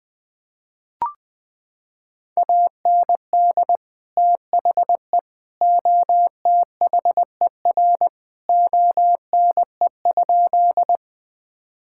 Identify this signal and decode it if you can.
Morse code, heard as a keyed tone, about 20 wpm, 700 Hz. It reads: AND THE OTHER ONE?
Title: Sentences with 3-Word Phrases - 20wpm